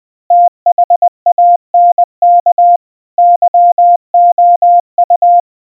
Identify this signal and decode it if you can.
Morse code, heard as a keyed tone, about 20 wpm, 700 Hz. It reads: THANK YOU
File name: morse / thank-you